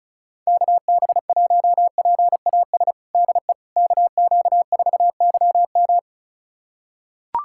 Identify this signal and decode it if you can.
Morse code, heard as a keyed tone, about 34 wpm, 700 Hz. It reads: KB1PAS DE KQ4YM
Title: USA Callsign DE Callsign 35wpm